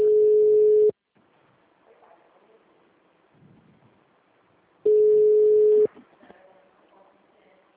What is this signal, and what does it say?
Ringback tone (Europe, 425 Hz), with speech.